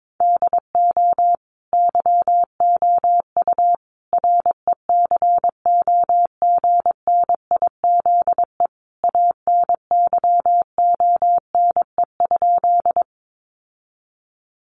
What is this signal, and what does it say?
Morse code, heard as a keyed tone, about 22 wpm, 700 Hz. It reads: DO YOU RECOGNIZE ANYONE?
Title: Sentences from Top 1400 Words 22wpm